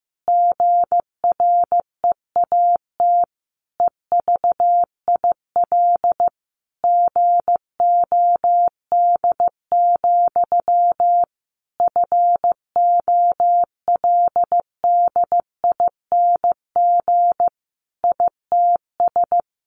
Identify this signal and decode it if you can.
Morse code, heard as a keyed tone, about 15 wpm, 700 Hz. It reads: GREAT EVIL GOD, FOLDING ITS
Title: A Sound Of Thunder 15wpm